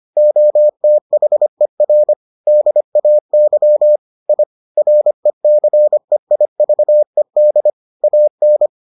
Morse code, heard as a keyed tone, about 25 wpm, 600 Hz. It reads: OTHER DAY I RECEIVED AN